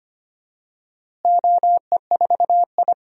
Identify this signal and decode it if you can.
Morse code, heard as a keyed tone, about 25 wpm, 700 Hz. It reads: OE4I
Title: DXCC-100 Call Signs - 4 Characters - Repeated 3x - 25wpm